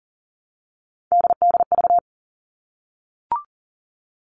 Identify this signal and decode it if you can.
Morse code, heard as a keyed tone, about 40 wpm, 700 Hz. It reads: DDV
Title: All Cut Numbers - 3-Digit 40wpm